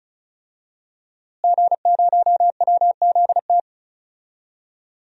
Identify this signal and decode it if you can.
Morse code, heard as a keyed tone, about 34 wpm, 700 Hz. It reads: G0WZT
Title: DXCC-100 Call Signs - 5 Characters - Repeated 2x - 35wpm